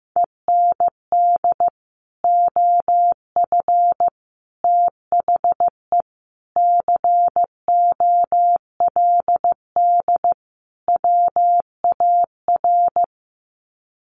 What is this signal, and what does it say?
Morse code, heard as a keyed tone, about 15 wpm, 700 Hz. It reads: END OF THE COLD WAR
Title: Sets of 5 Words Encore 15wpm